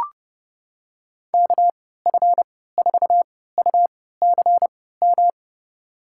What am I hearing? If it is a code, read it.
Morse code, heard as a keyed tone, about 30 wpm, spaced as for 17 wpm, 700 Hz. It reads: KF4UCM
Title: USA Call Signs - 3x Character Spacing - 30wpm